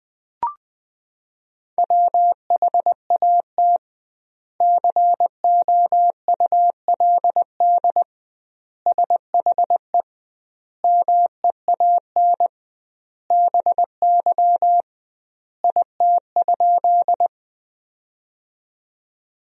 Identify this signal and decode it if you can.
Morse code, heard as a keyed tone, about 20 wpm, 700 Hz. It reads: WHAT COULD SHE MEAN BY IT?
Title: 2x Word-Spacing - Sentences from Top 200 Words - 20wpm